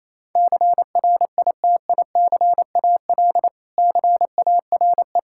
Morse code, heard as a keyed tone, about 28 wpm, 700 Hz. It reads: CRITICAL CARE
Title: Practice QSO - Exchange 3 - More Info - 28wpm